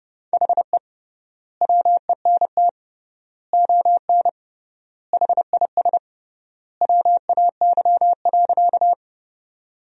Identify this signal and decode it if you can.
Morse code, heard as a keyed tone, about 30 wpm, 700 Hz. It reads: HE WENT ON HIS WAY.
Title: 3x Word-Spacing - Sentences from Top 200 Words - 30wpm